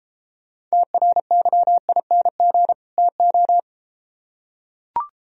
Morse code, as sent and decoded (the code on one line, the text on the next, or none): - .-. -.-- .. -. --. / - ---
TRYING TO